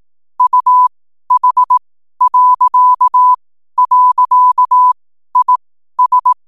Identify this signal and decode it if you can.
Morse code, heard as a keyed tone, about 18 wpm, spaced as for 12 wpm, 1000 Hz. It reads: UH..IS
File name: n7net / 12wpm